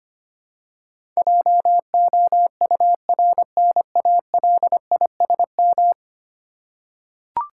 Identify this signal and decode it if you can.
Morse code, heard as a keyed tone, about 25 wpm, 690 Hz. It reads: JOURNALISM
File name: 10 Letter Words - First 3 Letters Given - 25wpm